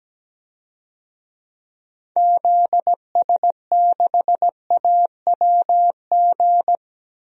Morse code, heard as a keyed tone, about 17 wpm, 700 Hz. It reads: ZS6AWG
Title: DXCC-100 Call Signs - 6 Characters - 17wpm